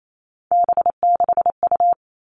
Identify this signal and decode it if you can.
Morse code, heard as a keyed tone, about 28 wpm, 700 Hz. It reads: B6U
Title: Three Challenging Letter-Number Combinations 28wpm